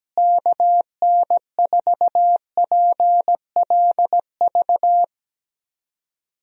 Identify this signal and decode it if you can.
Morse code, heard as a keyed tone, about 17 wpm, 700 Hz. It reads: KN4PLV
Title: USA Call Signs - 6 Characters - Repeated 3x - 17wpm